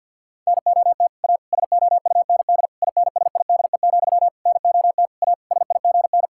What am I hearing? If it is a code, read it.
Morse code, heard as a keyed tone, about 52 wpm, 700 Hz. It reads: NOT A SOUND INSIDE, NOT A SIGN
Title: Sentences from Top 1000 Words - Review Part 6 - 50wpm